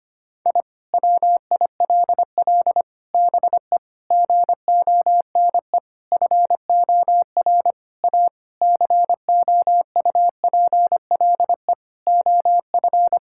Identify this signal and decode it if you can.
Morse code, heard as a keyed tone, about 25 wpm, 700 Hz. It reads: I WILL BE GONE FOR A COUPLE OF
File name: practice / 25wpm / Sentences with 4-Word Phrases - 25wpm